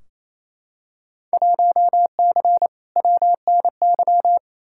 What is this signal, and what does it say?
Morse code, heard as a keyed tone, about 28 wpm, 700 Hz. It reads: 1C WNY
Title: Field Day Exchanges Encore - 28wpm